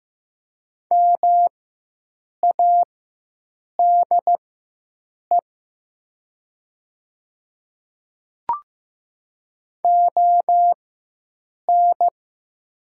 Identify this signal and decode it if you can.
Morse code, heard as a keyed tone, about 15 wpm, spaced as for 7 wpm, 700 Hz. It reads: MADE ON
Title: 4x Character Spacing - Top 300 Words - 15wpm